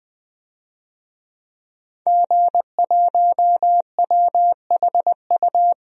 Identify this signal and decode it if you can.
Morse code, heard as a keyed tone, about 20 wpm, 700 Hz. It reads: G1WHU
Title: DXCC-100 Call Signs - 5 Characters - 20wpm